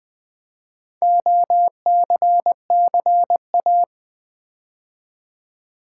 Morse code, heard as a keyed tone, about 20 wpm, 700 Hz. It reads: OCCA
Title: Build A Word - 4 Letters At A Time - 10-Letter Words - 20wpm